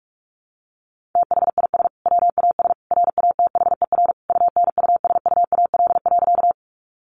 Morse code, heard as a keyed tone, about 45 wpm, 700 Hz. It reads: THIS WAS RATHER UNUSUAL.
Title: Sentences from Top 1400 Words 45wpm